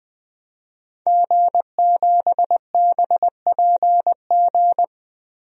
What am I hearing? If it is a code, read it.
Morse code, heard as a keyed tone, about 20 wpm, 700 Hz. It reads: G7BPG
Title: DXCC-100 Call Signs - 5 Characters - Repeated 2x - 20wpm